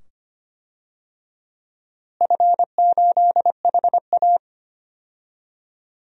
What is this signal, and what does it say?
Morse code, heard as a keyed tone, about 25 wpm, 700 Hz. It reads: F8HA
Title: DXCC-100 Call Signs - 4 Characters - Repeated 2x - 25wpm